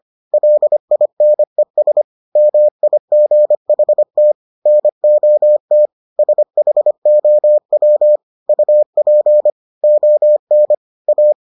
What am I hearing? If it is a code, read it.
Morse code, heard as a keyed tone, about 25 wpm, 600 Hz. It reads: LINES MIGHT NOT SHOW UP ON A